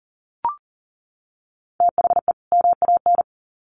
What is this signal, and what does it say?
Morse code, heard as a keyed tone, about 38 wpm, 700 Hz. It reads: THE MAN
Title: Sets of 2 Words 40wpm